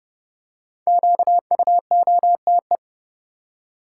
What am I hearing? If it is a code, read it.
Morse code, heard as a keyed tone, about 30 wpm, 700 Hz. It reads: QUOTE
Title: Five Letter Words - 30wpm